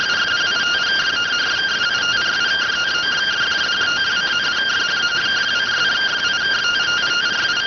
Two-tone teleprinter signal, about 170 Hz shift.